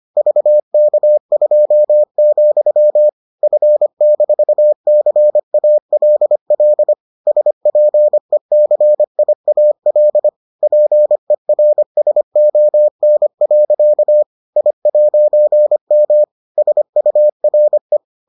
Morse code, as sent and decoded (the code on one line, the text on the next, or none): ...- -.- ..--- --..-- / ..-. -....- -.-. .- .-.. .-.. / ... .--. -.-. .. .- .-.. / .--. .-. ... --- -. .-.-.- / .. .----. -- / ... ..- .-. .
VK2, F-CALL SPECIAL PERSON. I'M SURE